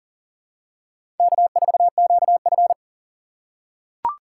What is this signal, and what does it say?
Morse code, heard as a keyed tone, about 42 wpm, 700 Hz. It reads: K4QF
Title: USA Call Signs - 6x Character Spacing - 40wpm